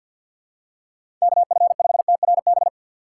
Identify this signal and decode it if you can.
Morse code, heard as a keyed tone, about 53 wpm, 700 Hz. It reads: KF5TRB
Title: USA Call Signs - 6 Characters - 8x Character Spacing - 50wpm